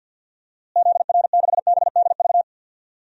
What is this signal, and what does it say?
Morse code, heard as a keyed tone, about 52 wpm, 700 Hz. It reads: ZR6BDV
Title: DXCC-100 Call Signs - 6 Characters - Repeated 2x - 50wpm